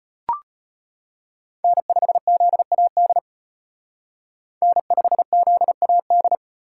Morse code, heard as a keyed tone, about 30 wpm, 700 Hz. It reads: N5ZAD N5ZAD
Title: Speed Racing - USA 1x3 Call Signs 25wpm